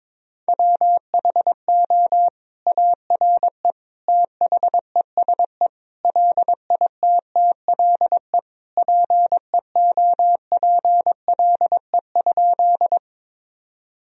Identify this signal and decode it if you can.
Morse code, heard as a keyed tone, about 22 wpm, 700 Hz. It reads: WHO ARE THESE LITTLE PEOPLE?
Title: Sentences from Top 1000 Words - Review Part 1 - 22wpm